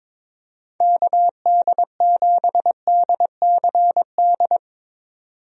Morse code, heard as a keyed tone, about 23 wpm, 700 Hz. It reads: KD7DCD
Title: USA Call Signs - 6 Characters - 22wpm